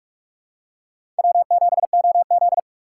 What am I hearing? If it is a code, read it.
Morse code, heard as a keyed tone, about 45 wpm, 700 Hz. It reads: W7OZ
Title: USA Call Signs - 4 Characters - 4x Character Spacing - 45wpm